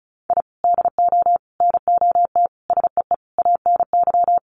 Morse code, heard as a keyed tone, about 34 wpm, 700 Hz. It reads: I DO NOT SEE ANY
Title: Sets of 5 Words 35wpm